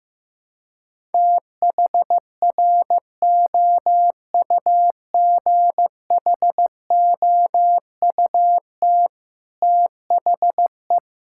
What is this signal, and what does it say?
Morse code, heard as a keyed tone, about 15 wpm, 700 Hz. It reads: THROUGHOUT THE
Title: Sentences from Top 1000 Words - Review Part 5 - 15wpm